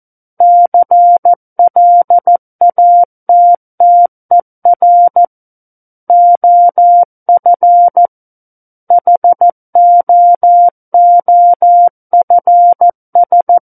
Morse code, heard as a keyed tone, about 14 wpm, 700 Hz. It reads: CLATTER OF HOOFS
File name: Book One / 05-Chapter 5 - 14 WPM